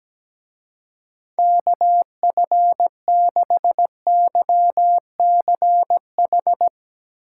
Morse code, heard as a keyed tone, about 19 wpm, 700 Hz. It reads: KF6YCH
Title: Speed Racing - USA Call Signs 17wpm